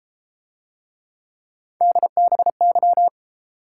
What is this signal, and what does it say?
Morse code, heard as a keyed tone, about 28 wpm, 700 Hz. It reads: DBY